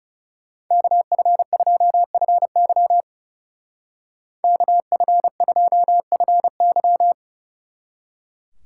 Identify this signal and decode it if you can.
Morse code, heard as a keyed tone, about 37 wpm, 700 Hz. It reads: KF2FY KF2FY
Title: Speed Racing - USA 2x2 Call Signs 30wpm